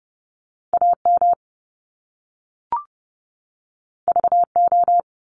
Morse code, heard as a keyed tone, about 30 wpm, 700 Hz. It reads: AM VO
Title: Top Two Letter Combinations in Words 30wpm